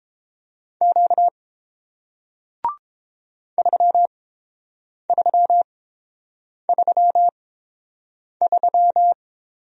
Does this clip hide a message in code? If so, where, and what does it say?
Yes: Morse code from 0.8 to 9.1 s, heard as a keyed tone, about 30 wpm, spaced as for 7 wpm, 700 Hz. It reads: Q 3333